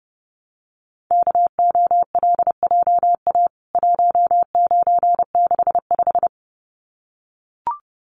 Morse code, heard as a keyed tone, about 30 wpm, 700 Hz. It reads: KOLJA 1965
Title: CWT Exchange 30wpm